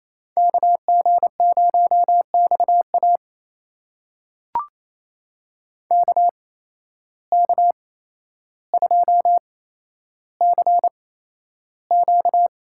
Morse code, heard as a keyed tone, about 28 wpm, spaced as for 12 wpm, 700 Hz. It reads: KG0XA KK2CQ
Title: USA Call Signs - 5 Characters - 8x Character Spacing - 28wpm